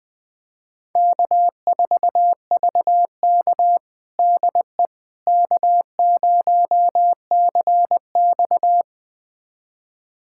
Morse code, heard as a keyed tone, about 20 wpm, 700 Hz. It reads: K4VK DE K0CX
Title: USA Callsign DE Callsign - 4 Characters - 20wpm